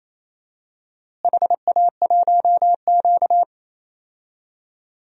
Morse code, heard as a keyed tone, about 28 wpm, 700 Hz. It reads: HA1Q
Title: DXCC-100 Call Signs - 4 Characters - Repeated 3x - 28wpm